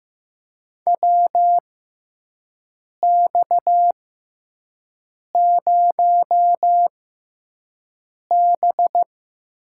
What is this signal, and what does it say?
Morse code, heard as a keyed tone, about 15 wpm, spaced as for 5 wpm, 700 Hz. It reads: WX0B
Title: USA Call Signs - 4 Characters - 6x Character Spacing - 15wpm